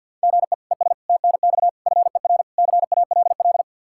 Morse code, heard as a keyed tone, about 49 wpm, 700 Hz. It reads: GE ES TNX FER CALL
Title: Practice QSO - Exchange 1 - Beginning - 50wpm